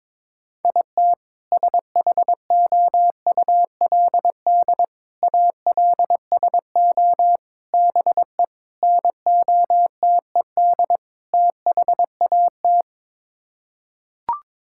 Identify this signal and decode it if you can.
Morse code, heard as a keyed tone, about 21 wpm, 700 Hz. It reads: IT SHOULD ALSO BE NOTED THAT
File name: practice / 22wpm / Sets of 6 Words - 22wpm